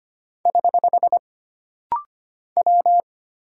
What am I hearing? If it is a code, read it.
Morse code, heard as a keyed tone, about 25 wpm, 700 Hz. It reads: <HH> W
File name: Single Character (Letter:Number:Punctuation:Prosign) - Rapid-Fire - 25wpm